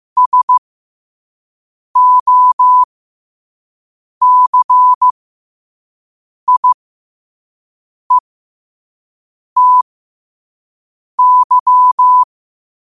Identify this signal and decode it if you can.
Morse code, heard as a keyed tone, about 15 wpm, spaced as for 5 wpm, 1000 Hz. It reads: SOCIETY
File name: 5wpm